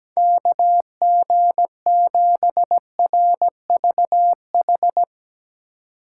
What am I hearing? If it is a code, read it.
Morse code, heard as a keyed tone, about 17 wpm, 700 Hz. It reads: KG7RVH
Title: USA Call Signs - 6 Characters - 3x Character Spacing - 17wpm